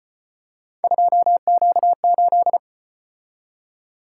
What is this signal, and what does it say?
Morse code, heard as a keyed tone, about 34 wpm, 700 Hz. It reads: 2Q8